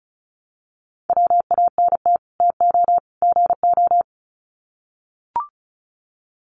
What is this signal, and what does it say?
Morse code, heard as a keyed tone, about 34 wpm, 700 Hz. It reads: WANT TO GO
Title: Sets of 3 Words 35wpm